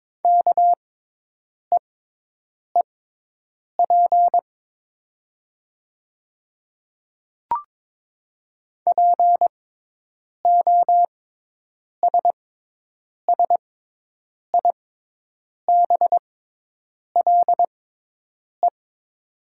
Morse code, heard as a keyed tone, about 22 wpm, spaced as for 8 wpm, 700 Hz. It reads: KEEP POSSIBLE
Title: 6x Character Spacing - Top 500 Words - 22wpm